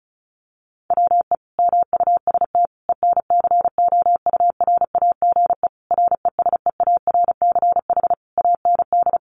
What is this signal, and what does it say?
Morse code, heard as a keyed tone, about 34 wpm, 700 Hz. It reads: WE MUST ENCOURAGE RESEARCH AND
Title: Sentences from Top 1400 Words 35wpm